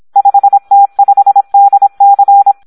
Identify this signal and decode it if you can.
Morse code, heard as a keyed tone, about 26 wpm, 800 Hz. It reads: 5T5DC
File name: Audio practica CW / SD 1 26